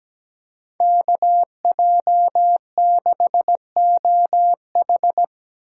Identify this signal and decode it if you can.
Morse code, heard as a keyed tone, about 20 wpm, 700 Hz. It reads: KJ6OH
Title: Speed Racing - USA 2x2 Call Signs 17wpm